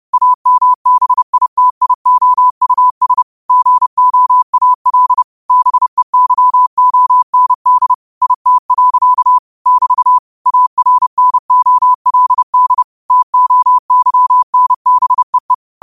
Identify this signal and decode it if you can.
Morse code, heard as a keyed tone, about 30 wpm, 1000 Hz. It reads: AMBITIOUS GOAL BEYOND IT. <BT> ARNOLD TOYNBEE
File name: qotdfiles / 30wpm